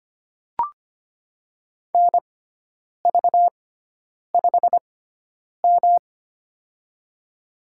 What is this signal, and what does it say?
Morse code, heard as a keyed tone, about 25 wpm, spaced as for 9 wpm, 700 Hz. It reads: NV5M
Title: USA Call Signs - 4 Characters - 6x Character Spacing - 25wpm